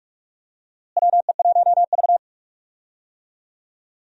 Morse code, heard as a keyed tone, about 46 wpm, 700 Hz. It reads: WE1V
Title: USA Call Signs - 4 Characters - Repeated 3x - 45wpm